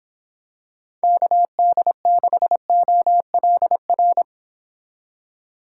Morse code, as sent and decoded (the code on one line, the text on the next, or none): -.- -.. -.... --- .-.. .-.
KD6OLR